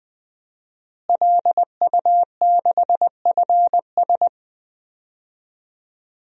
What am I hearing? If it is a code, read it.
Morse code, heard as a keyed tone, about 20 wpm, 700 Hz. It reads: LU6FS
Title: DXCC-100 Call Signs - 5 Characters - Repeated 2x - 20wpm